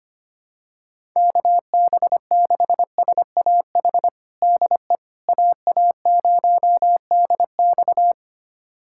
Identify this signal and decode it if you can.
Morse code, heard as a keyed tone, about 25 wpm, 700 Hz. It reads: KB6SAH DE AA0DX
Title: USA Callsign DE Callsign 25wpm